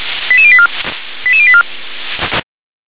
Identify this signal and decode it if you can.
Five-tone selective call (ZVEI), heard as quick runs of five tones, tones of about 70 ms. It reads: 88084 88084